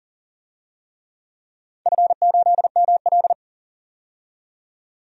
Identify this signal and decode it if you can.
Morse code, heard as a keyed tone, about 40 wpm, 700 Hz. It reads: F8ML
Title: DXCC-100 Call Signs - 4 Characters - 40wpm